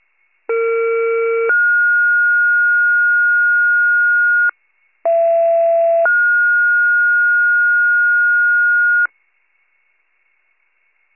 Two-tone page, heard as a one-second tone, then a three-second tone, about 460 then 1430 Hz, about 670 then 1430 Hz. It is heard from 0.5 to 9.1 s.
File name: pager / 21 FIRE / wcalert